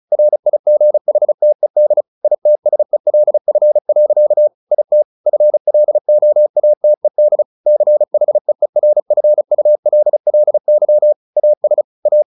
Morse code, heard as a keyed tone, about 34 wpm, 600 Hz. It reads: RIGHTED ITSELF. IT FLOATED CHEERFULLY AS A
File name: the old man and the sea / the old man and the sea 1 35wpm